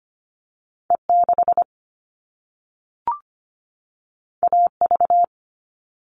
Morse code, heard as a keyed tone, about 25 wpm, 700 Hz. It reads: E 6 A V
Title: All Cut Numbers - 2-Digit 25wpm